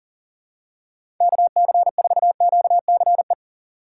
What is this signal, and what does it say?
Morse code, heard as a keyed tone, about 40 wpm, 700 Hz. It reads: KC4QCE
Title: USA Call Signs - 6 Characters - 3x Character Spacing - 40wpm